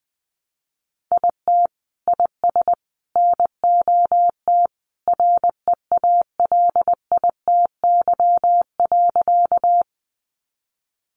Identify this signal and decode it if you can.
Morse code, heard as a keyed tone, about 20 wpm, 700 Hz. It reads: IT IS NOT REALITY.
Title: Sentences from Top 1200 Words 20wpm